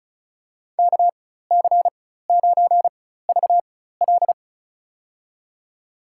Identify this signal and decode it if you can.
Morse code, heard as a keyed tone, about 34 wpm, spaced as for 16 wpm, 700 Hz. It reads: KC9VL
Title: USA Call Signs - 5 Characters - 4x Character Spacing - 35wpm